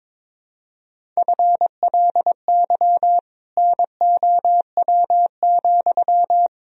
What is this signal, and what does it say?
Morse code, heard as a keyed tone, about 22 wpm, 700 Hz. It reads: FLY NOW,
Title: Sentences from Top 1000 Words - Review Part 2 - 22wpm